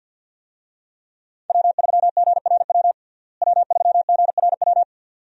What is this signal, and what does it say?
Morse code, heard as a keyed tone, about 50 wpm, 700 Hz. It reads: W3GRW W3GRW
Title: USA Call Signs - 5 Characters - Repeated 2x - 50wpm